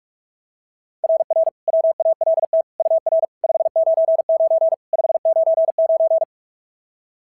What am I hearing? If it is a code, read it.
Morse code, heard as a keyed tone, about 44 wpm, 690 Hz. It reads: RR WALT UR 599 599
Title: SOTA QSO Practice 45wpm